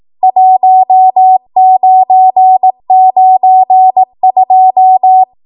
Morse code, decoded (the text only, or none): 1992